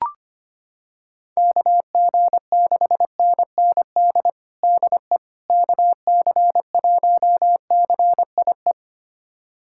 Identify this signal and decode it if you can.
Morse code, heard as a keyed tone, about 25 wpm, 700 Hz. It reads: KG6NND DE KC1CIE
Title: USA Callsign DE Callsign - 6 Characters - 25wpm